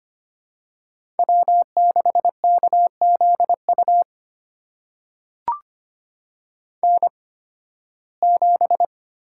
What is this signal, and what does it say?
Morse code, heard as a keyed tone, about 25 wpm, spaced as for 16 wpm, 700 Hz. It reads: W6KZU N7
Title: USA Call Signs - 5 Characters - 8x Character Spacing - 25wpm